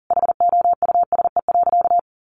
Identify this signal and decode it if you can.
Morse code, heard as a keyed tone, about 40 wpm, 700 Hz. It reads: HOUSE.
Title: Sentences from Top 1100 Words 40wpm